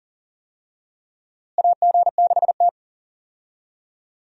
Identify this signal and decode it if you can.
Morse code, heard as a keyed tone, about 40 wpm, 700 Hz. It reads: AG6T
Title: USA Call Signs - 4 Characters - Repeated 2x - 40wpm